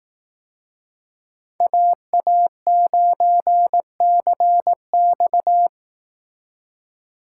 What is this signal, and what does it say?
Morse code, heard as a keyed tone, about 18 wpm, 700 Hz. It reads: AA9CX
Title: Speed Racing - USA 2x2 Call Signs 15wpm